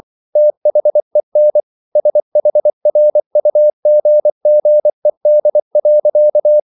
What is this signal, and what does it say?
Morse code, heard as a keyed tone, about 24 wpm, 600 Hz. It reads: THEN SHRUGGED.